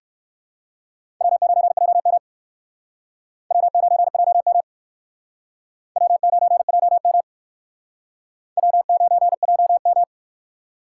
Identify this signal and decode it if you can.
Morse code, heard as a keyed tone, about 56 wpm, 700 Hz. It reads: W9JM W9JM W9JM W9JM
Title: Speed Racing - USA 1x2 Call Signs 45wpm